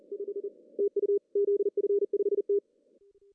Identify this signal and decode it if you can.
Morse code, heard as a keyed tone, about 40 wpm, 400 Hz. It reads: 5 TU ZF5T
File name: CW / cqww-cw-k5tr